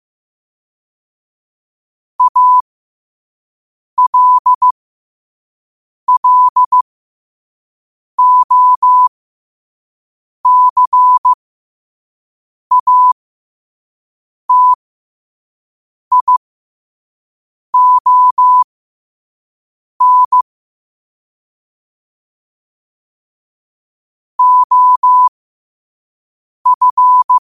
Morse code, heard as a keyed tone, about 15 wpm, spaced as for 5 wpm, 1000 Hz. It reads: ALLOCATION OF
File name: qotdfiles / 5wpm